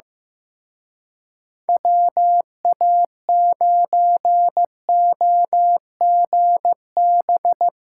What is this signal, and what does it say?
Morse code, heard as a keyed tone, about 17 wpm, 700 Hz. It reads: WA9OGB
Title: Speed Racing - USA 2x3 Call Signs 15wpm